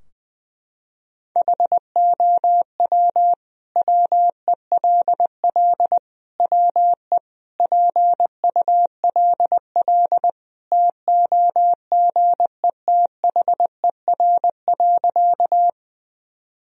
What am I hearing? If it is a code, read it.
Morse code, heard as a keyed tone, about 20 wpm, 700 Hz. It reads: HOW WELL WE PULL TOGETHER.
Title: Sentences from Top 1000 Words - Review Part 2 - 20wpm